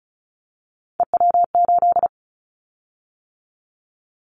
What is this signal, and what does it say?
Morse code, heard as a keyed tone, about 36 wpm, 700 Hz. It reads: EW8
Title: Callsign Prefixes for Top 50 DXCC - 35wpm